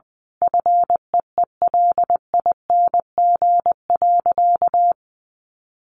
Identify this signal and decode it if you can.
Morse code, heard as a keyed tone, about 20 wpm, 700 Hz. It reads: FEELING.
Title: Sentences from Top 1400 Words 20wpm